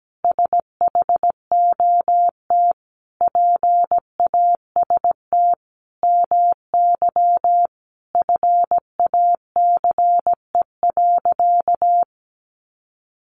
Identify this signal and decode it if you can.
Morse code, heard as a keyed tone, about 17 wpm, 700 Hz. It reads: SHOT PAST MY FACE.
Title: Sentences from Top 1100 Words 17wpm